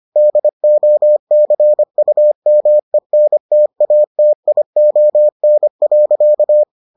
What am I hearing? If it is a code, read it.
Morse code, heard as a keyed tone, about 25 wpm, 600 Hz. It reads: DOCUMENTATION.